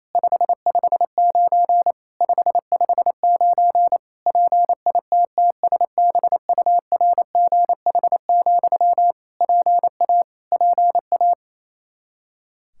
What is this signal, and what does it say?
Morse code, heard as a keyed tone, about 28 wpm, 700 Hz. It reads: 559 559 PITTSBURGH, PA PA
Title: Practice QSO - Exchange 1 - Beginning - 28wpm